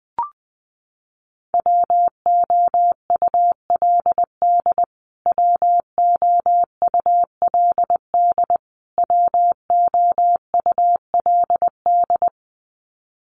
Morse code, heard as a keyed tone, about 20 wpm, 700 Hz. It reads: WOULD WOULD WOULD
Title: Top 300 Words Repeated 20wpm